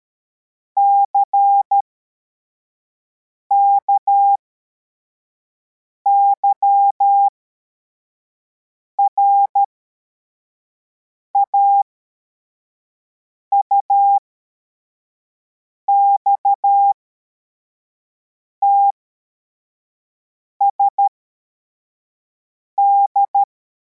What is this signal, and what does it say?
Morse code, heard as a keyed tone, about 13 wpm, spaced as for 4 wpm, 800 Hz. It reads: CKYRA UXTSD
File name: formation / cw / Text64